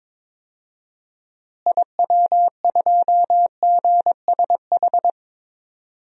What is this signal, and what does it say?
Morse code, heard as a keyed tone, about 22 wpm, 700 Hz. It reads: IW2GSH
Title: DXCC-100 Call Signs - 6 Characters - Repeated 3x - 22wpm